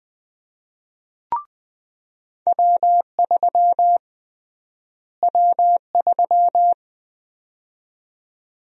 Morse code, heard as a keyed tone, about 20 wpm, 700 Hz. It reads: W3 W3
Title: USA Call Sign Prefixes - Repeated 2x - 20wpm